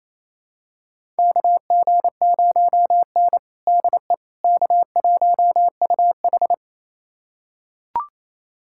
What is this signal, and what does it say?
Morse code, heard as a keyed tone, about 27 wpm, 700 Hz. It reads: KG0N DE K1UH
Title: USA Callsign DE Callsign - 4 Characters - 28wpm